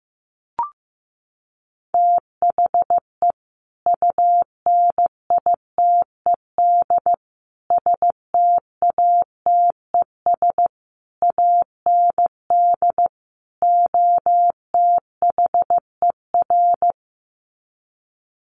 Morse code, heard as a keyed tone, about 15 wpm, 700 Hz. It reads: THE UNITED STATES AND OTHER
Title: Sets of 5 Words Encore 15wpm